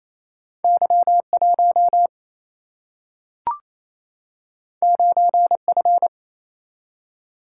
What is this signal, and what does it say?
Morse code, heard as a keyed tone, about 28 wpm, 700 Hz. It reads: Y1 9F
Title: Single-Letter Single-Number 28wpm